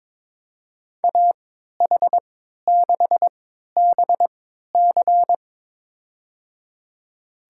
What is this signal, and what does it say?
Morse code, heard as a keyed tone, about 22 wpm, spaced as for 12 wpm, 700 Hz. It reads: AH6BC
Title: USA Call Signs - 5 Characters - 3x Character Spacing - 22wpm